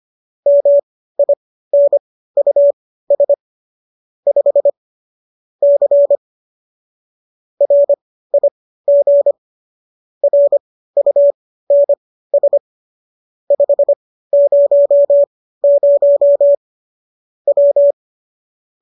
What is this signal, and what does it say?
Morse code, heard as a keyed tone, about 25 wpm, spaced as for 15 wpm, 570 Hz. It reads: MINUS 5 C RIG RUNS 500 W
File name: qso203 15